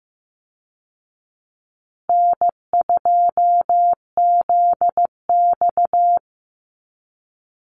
Morse code, heard as a keyed tone, about 15 wpm, 700 Hz. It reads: N2ZX
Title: Call Signs Repeated 3x - 15wpm